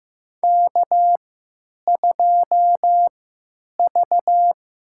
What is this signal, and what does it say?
Morse code, heard as a keyed tone, about 15 wpm, spaced as for 9 wpm, 700 Hz. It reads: K2V
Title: USA Call Signs - 3x Character Spacing - 15wpm